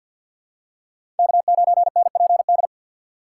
Morse code, heard as a keyed tone, about 51 wpm, 700 Hz. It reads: K9NPD
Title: USA Call Signs - 5 Characters - 6x Character Spacing - 50wpm